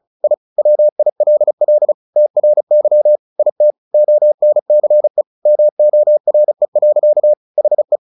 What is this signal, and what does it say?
Morse code, heard as a keyed tone, about 34 wpm, 600 Hz. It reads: I WILL TRY IT ONCE MORE. HE
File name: the old man and the sea 2 35wpm